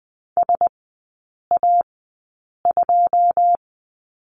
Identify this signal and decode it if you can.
Morse code, heard as a keyed tone, about 20 wpm, spaced as for 8 wpm, 700 Hz. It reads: SA2
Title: 107 Lesson 18e - Letter m - Three Characters 20wpm